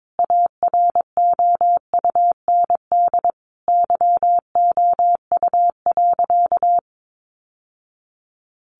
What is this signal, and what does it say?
Morse code, heard as a keyed tone, about 22 wpm, 700 Hz. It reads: AROUND YOU.
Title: Sentences from Top 1200 Words 22wpm